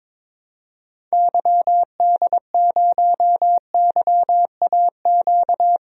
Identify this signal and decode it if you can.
Morse code, heard as a keyed tone, about 22 wpm, 700 Hz. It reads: YD0YAQ
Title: DXCC-100 Call Signs - 6 Characters - 22wpm